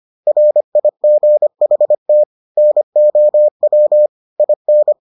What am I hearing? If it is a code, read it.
Morse code, heard as a keyed tone, about 25 wpm, 600 Hz. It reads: RIGHT NOW IN